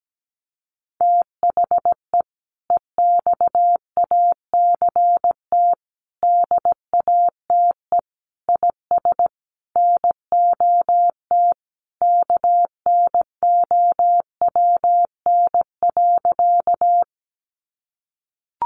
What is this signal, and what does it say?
Morse code, heard as a keyed tone, about 17 wpm, 700 Hz. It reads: THE EXACT DATE IS NOT KNOWN.
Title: Sentences from Top 1300 Words 17wpm